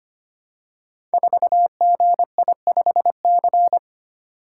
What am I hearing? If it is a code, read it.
Morse code, heard as a keyed tone, about 25 wpm, 700 Hz. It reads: 4GI5C
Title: Code Groups - 5 Letter:Numbers - 25wpm